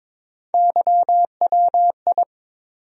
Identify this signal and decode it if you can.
Morse code, heard as a keyed tone, about 22 wpm, 700 Hz. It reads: YWI
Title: USA Call Sign Postfixes - 3 Letters 22wpm